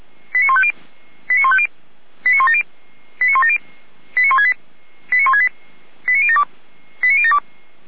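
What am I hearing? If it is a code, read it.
Five-tone selective call (ZVEI), heard as quick runs of five tones, tones of about 70 ms. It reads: 89150 89150 89170 89170 89168 89168 89072 89072
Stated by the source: ZVEI five-tone selective call received on an SDR.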